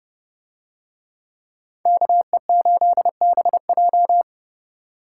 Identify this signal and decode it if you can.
Morse code, heard as a keyed tone, about 28 wpm, 700 Hz. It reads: KE8BJ